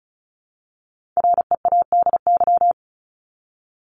Morse code, heard as a keyed tone, about 34 wpm, 700 Hz. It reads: READY
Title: Top 500 Words 35wpm